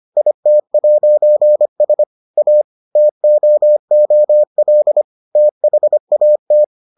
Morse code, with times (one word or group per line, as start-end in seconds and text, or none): 0.2-2.0 IT'S
2.4-2.6 A
3.0-5.0 TOOL
5.4-6.6 THAT